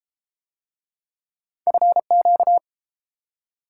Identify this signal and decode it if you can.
Morse code, heard as a keyed tone, about 31 wpm, 700 Hz. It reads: FQ